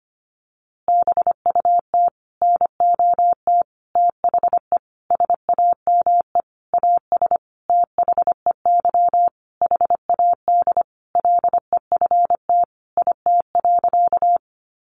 Morse code, heard as a keyed tone, about 25 wpm, 700 Hz. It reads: BUT NOT THE SAME AS THEY HAD LEFT IT.
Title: A Sound Of Thunder 25wpm